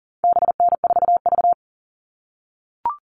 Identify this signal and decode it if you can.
Morse code, heard as a keyed tone, about 40 wpm, 700 Hz. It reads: BN4V
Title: All Cut Numbers - 4-Digit 40wpm